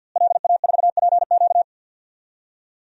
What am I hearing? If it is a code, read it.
Morse code, heard as a keyed tone, about 47 wpm, 700 Hz. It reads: LA4PQ
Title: DXCC-100 Call Signs - 5 Characters - Repeated 2x - 50wpm